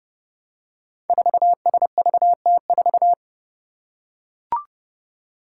Morse code, heard as a keyed tone, about 30 wpm, 700 Hz. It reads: 4SVT4
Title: Five Challenging Letter-Number Combinations 30wpm